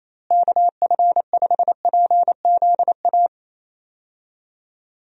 Morse code, heard as a keyed tone, about 28 wpm, 700 Hz. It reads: KF5PZA
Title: USA Call Signs - 6 Characters - Repeated 3x - 28wpm